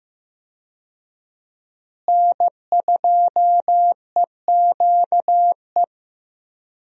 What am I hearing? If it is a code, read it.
Morse code, heard as a keyed tone, about 15 wpm, 700 Hz. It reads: N2EQE
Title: USA Call Signs - 5 Characters - Repeated 2x - 15wpm